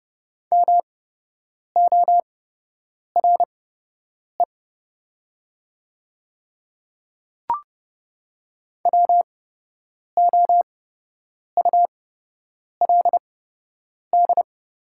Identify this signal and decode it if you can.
Morse code, heard as a keyed tone, about 30 wpm, spaced as for 8 wpm, 700 Hz. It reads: MORE WOULD
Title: 8x Character Spacing - Top 100 Words - 30wpm